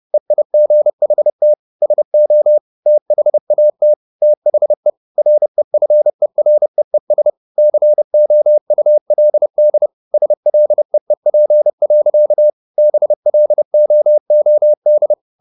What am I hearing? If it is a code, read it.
Morse code, heard as a keyed tone, about 30 wpm, 600 Hz. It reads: EIGHT SO THAT THE REFEREES COULD SLEEP. BLOOD
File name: the old man and the sea 2 30wpm